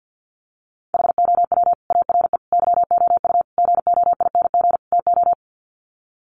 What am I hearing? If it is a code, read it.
Morse code, heard as a keyed tone, about 50 wpm, 700 Hz. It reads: HOW ARE YOU GOING TO
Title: Sets of 5 Words 50wpm